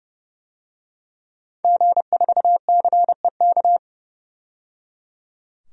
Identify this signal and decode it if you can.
Morse code, heard as a keyed tone, about 31 wpm, 700 Hz. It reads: G4CEK
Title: DXCC-100 Call Signs - 5 Characters - 30wpm